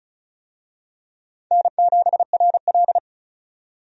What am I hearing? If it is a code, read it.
Morse code, heard as a keyed tone, about 34 wpm, 700 Hz. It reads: N7RL
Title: USA Call Signs - 4 Characters - 35wpm